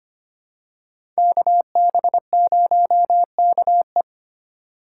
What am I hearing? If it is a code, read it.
Morse code, heard as a keyed tone, about 25 wpm, 700 Hz. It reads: KB0KE
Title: USA Call Signs - 5 Characters - 3x Character Spacing - 25wpm